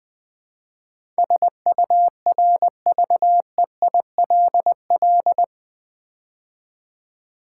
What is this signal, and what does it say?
Morse code, heard as a keyed tone, about 21 wpm, 700 Hz. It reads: SURVEILL
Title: Build A Word - 4 Letters At A Time - 12-Letter Words - 20wpm